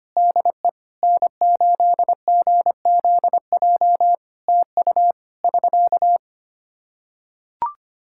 Morse code, heard as a keyed tone, about 25 wpm, 700 Hz. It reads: DE N8GZJ TU <SK>
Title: Practice QSO - Exchange 4 - Ending - 25wpm